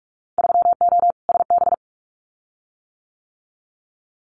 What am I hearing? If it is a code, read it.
Morse code, heard as a keyed tone, about 46 wpm, 700 Hz. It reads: 3O SB
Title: Winter Field Day Exchanges Encore 45wpm